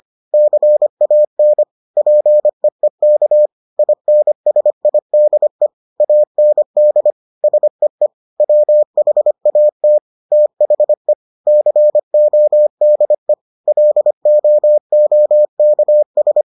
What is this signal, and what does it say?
Morse code, heard as a keyed tone, about 25 wpm, 600 Hz. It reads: CAN PEEK INSIDE AND SEE WHAT THE CODE LOOKS